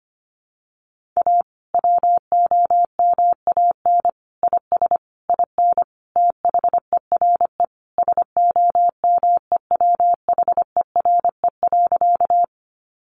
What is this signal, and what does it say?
Morse code, heard as a keyed tone, about 25 wpm, 700 Hz. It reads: A WOMAN IS IN THERE SOMEWHERE.
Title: Sentences from Top 1400 Words 25wpm